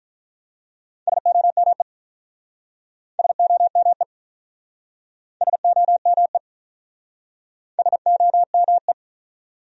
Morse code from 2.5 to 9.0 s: SOME SOME SOME

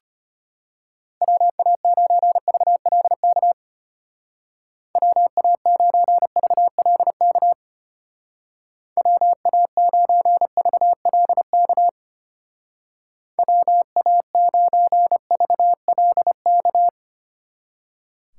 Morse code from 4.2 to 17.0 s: WA9VLK WA9VLK WA9VLK

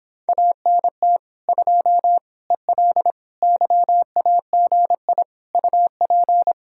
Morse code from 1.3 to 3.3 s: 2 EL

